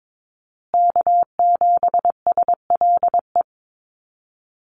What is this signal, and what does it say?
Morse code, heard as a keyed tone, about 22 wpm, 700 Hz. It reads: K7SLE
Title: USA Call Signs Repeated 3x 22wpm